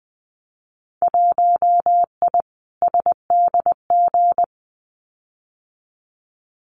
Morse code, heard as a keyed tone, about 20 wpm, 700 Hz. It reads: 1I SDG
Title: Winter Field Day Exchanges 20wpm